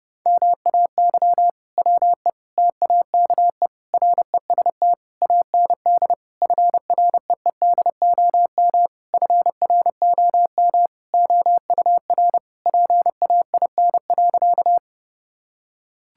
Morse code, heard as a keyed tone, about 30 wpm, 700 Hz. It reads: MAY WE TAKE REST AND FREEDOM FROM OUR PAIN.
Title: Sentences from Top 1600 Words 30wpm